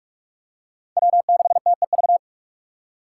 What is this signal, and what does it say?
Morse code, heard as a keyed tone, about 44 wpm, 700 Hz. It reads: W6TEV